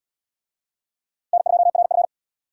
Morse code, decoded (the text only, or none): N2RF